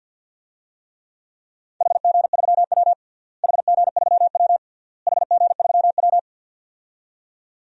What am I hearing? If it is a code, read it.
Morse code, heard as a keyed tone, about 52 wpm, 700 Hz. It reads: HG3W HG3W HG3W
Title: DXCC-100 Call Signs - 4 Characters - Repeated 3x - 50wpm